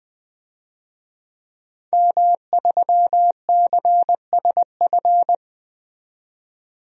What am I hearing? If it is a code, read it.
Morse code, heard as a keyed tone, about 21 wpm, 700 Hz. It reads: M3CSF
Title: DXCC-100 Call Signs - 5 Characters - 20wpm